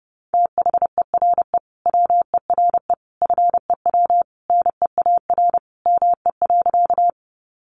Morse code, heard as a keyed tone, about 30 wpm, 700 Hz. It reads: THERE WERE FEW NEAR ME.
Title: Sentences from Top 300 Words 30wpm